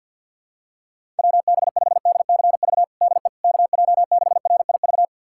Morse code, heard as a keyed tone, about 52 wpm, 700 Hz. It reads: WB5DCV DE KJ6RIV
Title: USA Callsign DE Callsign - 6 Characters - 50wpm